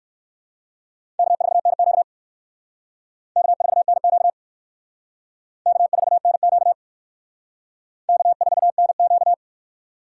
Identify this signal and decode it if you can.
Morse code, heard as a keyed tone, about 58 wpm, 700 Hz. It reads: K4NQ K4NQ K4NQ K4NQ